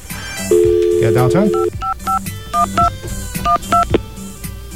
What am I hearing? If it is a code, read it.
DTMF, heard as telephone touch tones, tones of about 120 ms, with speech. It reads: 55512 12
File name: mp3files / off the hook -128